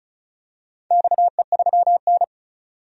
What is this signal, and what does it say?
Morse code, heard as a keyed tone, about 36 wpm, 700 Hz. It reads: XE3N
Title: DXCC-100 Call Signs - 4 Characters - 35wpm